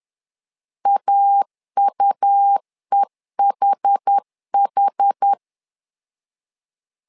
Morse code, heard as a keyed tone, about 10 wpm, 800 Hz. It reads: AUEHH
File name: LEZIONI / 2 UHDN / 2 UH 10wpm 1714